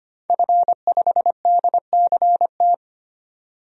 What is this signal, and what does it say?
Morse code, heard as a keyed tone, about 25 wpm, 700 Hz. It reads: F5DCT
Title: DXCC-100 Call Signs - 5 Characters - 25wpm